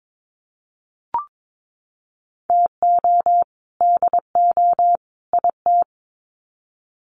Morse code, heard as a keyed tone, about 22 wpm, 700 Hz. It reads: TO DO IT
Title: Sets of 3 Words 22wpm